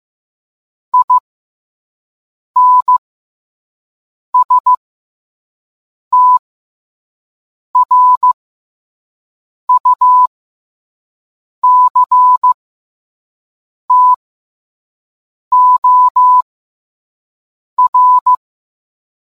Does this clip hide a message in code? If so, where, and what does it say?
Yes: Morse code from 0.9 to 18.3 s, heard as a keyed tone, about 15 wpm, spaced as for 5 wpm, 1000 Hz. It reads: INSTRUCTOR